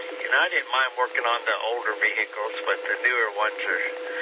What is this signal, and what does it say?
Emergency Alert System (SAME) tones, heard as harsh buzzing bursts, with speech. It is heard from 0.2 to 3.9 s.